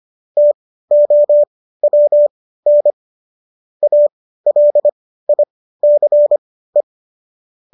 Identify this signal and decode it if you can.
Morse code, heard as a keyed tone, about 25 wpm, spaced as for 15 wpm, 590 Hz. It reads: TOWN ALICE
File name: qso204 15